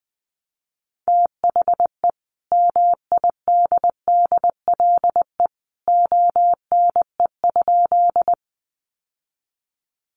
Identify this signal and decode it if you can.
Morse code, heard as a keyed tone, about 20 wpm, 700 Hz. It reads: THE MIDDLE ONE?
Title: Sentences from Top 900 Words 20wpm